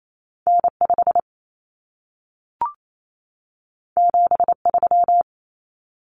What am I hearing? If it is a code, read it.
Morse code, heard as a keyed tone, about 28 wpm, 700 Hz. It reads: N5 73
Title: Common Cut Numbers - 2-Digit 28wpm